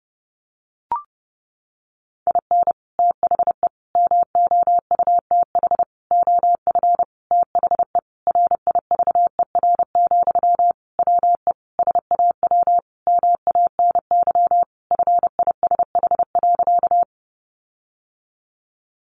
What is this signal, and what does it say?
Morse code, heard as a keyed tone, about 30 wpm, 700 Hz. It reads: IN THE MOUTH OF THE RIVER, WE SAW MANY FISH.
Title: Sentences from Top 1500 Words 30wpm